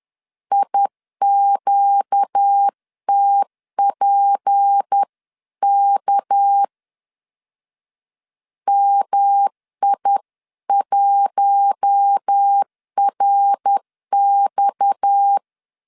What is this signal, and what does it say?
Morse code, heard as a keyed tone, about 10 wpm, 800 Hz. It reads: IQTPK MI1RX